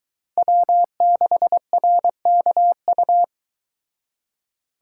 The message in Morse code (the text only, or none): W6RKU